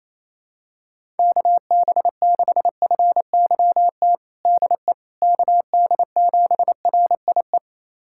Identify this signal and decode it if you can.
Morse code, heard as a keyed tone, about 27 wpm, 700 Hz. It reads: KB6FYT DE KD7RIE
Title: USA Callsign DE Callsign - 6 Characters - 28wpm